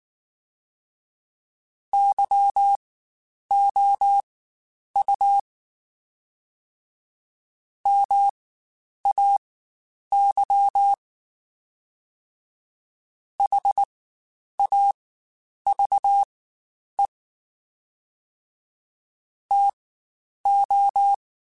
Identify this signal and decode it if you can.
Morse code, heard as a keyed tone, about 19 wpm, spaced as for 9 wpm, 800 Hz. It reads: YOU MAY HAVE TO
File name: Spaced / Chuckles Spaced-mp3s / SoftWords sp @18